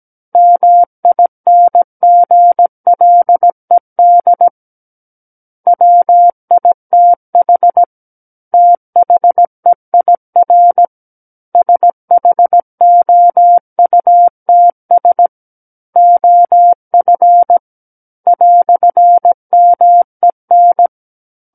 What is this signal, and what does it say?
Morse code, heard as a keyed tone, about 17 wpm, 700 Hz. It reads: MINGLED WITH THEIR SHOUTS OF "MEN
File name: Book One / 08-Chapter 8 - 17 WPM